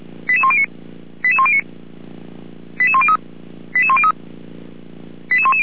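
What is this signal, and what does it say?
Five-tone selective call (ZVEI), heard as quick runs of five tones, tones of about 70 ms. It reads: 88109 88109 88103 88103 88110
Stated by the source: ZVEI five-tone selective call received on an SDR, from audio.f85360.feuerwehr-orlamuende.org.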